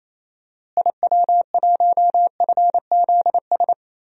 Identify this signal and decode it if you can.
Morse code, heard as a keyed tone, about 28 wpm, 700 Hz. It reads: IW1FZS
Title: DXCC-100 Call Signs - 6 Characters - Repeated 2x - 28wpm